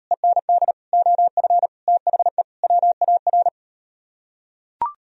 Morse code, heard as a keyed tone, about 38 wpm, 700 Hz. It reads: END OF THE WAR